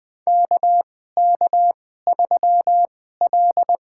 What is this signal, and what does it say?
Morse code, heard as a keyed tone, about 21 wpm, spaced as for 15 wpm, 700 Hz. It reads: KK3L
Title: USA Call Signs - 4 Characters - 2x Character Spacing - 20wpm